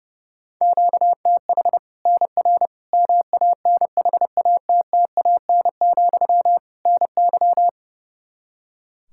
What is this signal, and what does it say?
Morse code, heard as a keyed tone, about 32 wpm, 700 Hz. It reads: QTH NR MANHATTAN, NY
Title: Practice QSO - Exchange 1 - Beginning - 30wpm